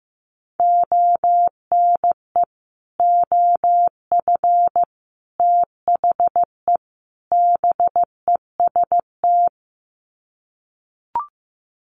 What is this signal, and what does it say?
Morse code, heard as a keyed tone, about 15 wpm, 700 Hz. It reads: ONE OF THE BEST
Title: Sets of 4 Words 15wpm